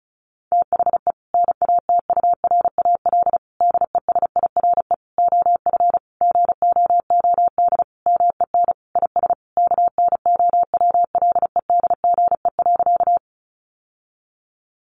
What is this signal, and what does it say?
Morse code, heard as a keyed tone, about 34 wpm, 700 Hz. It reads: THE NATURAL DESIRE OF GOOD MEN IS KNOWLEDGE.
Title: Sentences from Top 1100 Words 35wpm